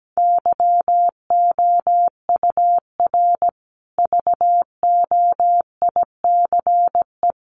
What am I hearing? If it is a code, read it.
Morse code, heard as a keyed tone, about 17 wpm, 700 Hz. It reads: YOUR VOICE
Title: Sentences from Top 500 Words 17wpm